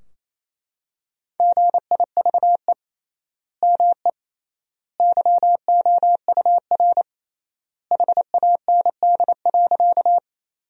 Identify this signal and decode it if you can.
Morse code, heard as a keyed tone, about 28 wpm, 700 Hz. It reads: GIVE ME YOUR HAND.
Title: 3x Word-Spacing - Sentences from Top 200 Words - 28wpm